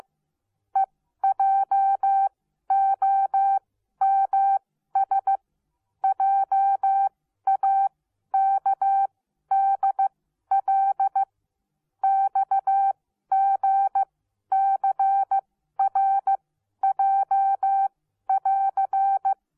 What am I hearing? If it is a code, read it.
Morse code, heard as a keyed tone, about 15 wpm, spaced as for 12 wpm, 780 Hz. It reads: EJOMS JAKDL XGCRJ<AR>